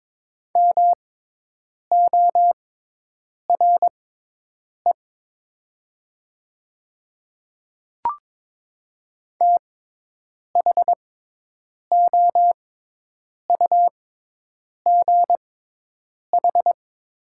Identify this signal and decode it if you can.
Morse code, heard as a keyed tone, about 22 wpm, spaced as for 8 wpm, 700 Hz. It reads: MORE THOUGH
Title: 6x Character Spacing - Top 500 Words - 22wpm